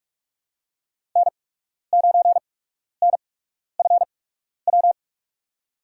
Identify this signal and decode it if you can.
Morse code, heard as a keyed tone, about 43 wpm, spaced as for 12 wpm, 700 Hz. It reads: N9NFW
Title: USA Call Signs - 5 Characters - 8x Character Spacing - 45wpm